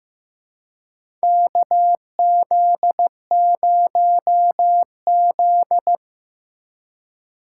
Morse code, heard as a keyed tone, about 16 wpm, 700 Hz. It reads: KZ0Z
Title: USA Call Signs - 4 Characters - 2x Character Spacing - 15wpm